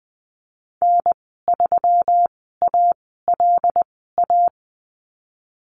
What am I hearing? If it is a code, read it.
Morse code, heard as a keyed tone, about 20 wpm, spaced as for 15 wpm, 700 Hz. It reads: N3ALA
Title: 122 Lesson 20g - Number 3 - Call Signs 20wpm